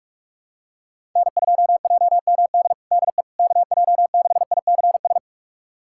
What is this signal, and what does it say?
Morse code, heard as a keyed tone, about 41 wpm, 690 Hz. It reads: N2JMD DE KJ6ICS
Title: USA Callsign DE Callsign 45wpm